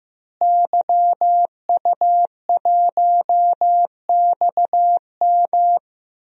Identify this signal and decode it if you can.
Morse code, heard as a keyed tone, about 15 wpm, 700 Hz. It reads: YU1XM
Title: DXCC-100 Call Signs - 5 Characters - Repeated 3x - 15wpm